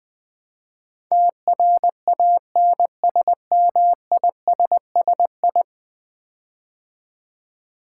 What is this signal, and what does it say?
Morse code, heard as a keyed tone, about 20 wpm, 700 Hz. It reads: TRANSMISSI
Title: Build A Word - 5 Letters At A Time - 12-Letter Words - 20wpm